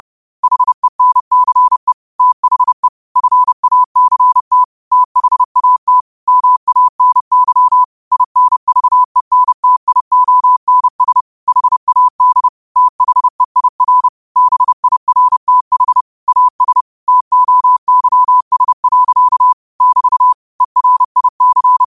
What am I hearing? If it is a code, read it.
Morse code, heard as a keyed tone, about 30 wpm, 1000 Hz. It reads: HENCE THE FACT THAT MANY INVENTIONS HAD THEIR BIRTH AS TOYS. <BT> ERIC